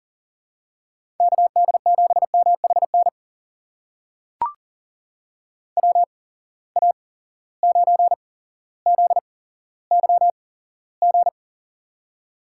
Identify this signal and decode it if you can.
Morse code, heard as a keyed tone, about 39 wpm, spaced as for 17 wpm, 700 Hz. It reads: KD7MHN WA9ZYG
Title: USA Call Signs - 6 Characters - 8x Character Spacing - 40wpm